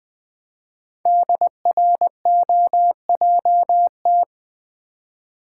Morse code, heard as a keyed tone, about 20 wpm, 700 Hz. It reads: DROJT
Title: Code Groups - 5 Letters - 20wpm